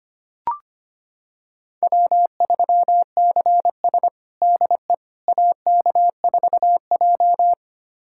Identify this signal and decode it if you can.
Morse code, heard as a keyed tone, about 25 wpm, 700 Hz. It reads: W3CS DE AK4J
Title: USA Callsign DE Callsign - 4 Characters - 25wpm